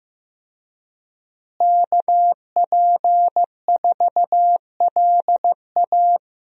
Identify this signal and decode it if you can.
Morse code, heard as a keyed tone, about 15 wpm, 700 Hz. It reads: KP4LA
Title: USA Call Signs - 5 Characters - Repeated 2x - 15wpm